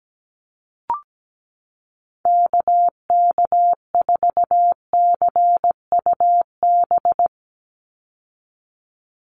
Morse code, heard as a keyed tone, about 17 wpm, 700 Hz. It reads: KK4CUB
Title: USA Call Signs 17wpm